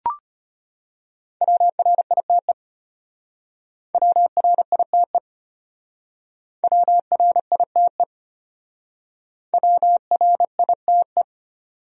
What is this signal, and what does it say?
Morse code, heard as a keyed tone, about 32 wpm, 700 Hz. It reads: WRITE WRITE WRITE WRITE